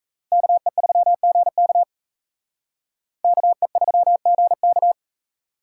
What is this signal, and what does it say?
Morse code, heard as a keyed tone, about 35 wpm, 700 Hz. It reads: KE3GK KE3GK